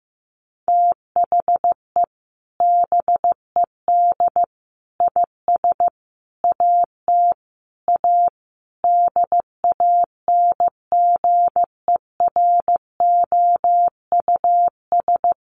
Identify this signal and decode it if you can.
Morse code, heard as a keyed tone, about 15 wpm, 700 Hz. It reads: THE BED IS AT A DANGEROUS
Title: Sentences from Top 1500 Words 15wpm